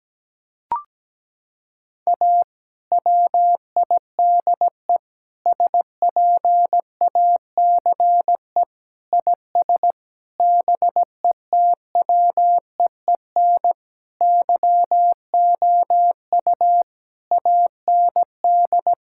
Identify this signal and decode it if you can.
Morse code, heard as a keyed tone, about 17 wpm, 700 Hz. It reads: A WIDE SPACE IS BETWEEN YOU AND
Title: Sentences from Top 1000 Words - Review Part 6 - 17wpm